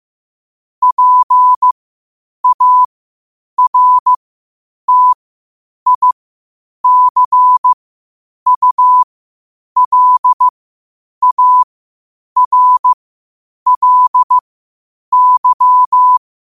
Morse code, heard as a keyed tone, about 15 wpm, spaced as for 8 wpm, 1000 Hz. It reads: PARTICULARLY
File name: qotdfiles / 8wpm